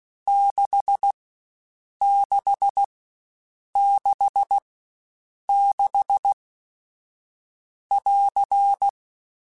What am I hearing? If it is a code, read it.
Morse code, heard as a keyed tone, about 16 wpm, spaced as for 7 wpm, 800 Hz. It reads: 6666<AR>
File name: Short Runs / Short Runs-mp3s / Staccato@15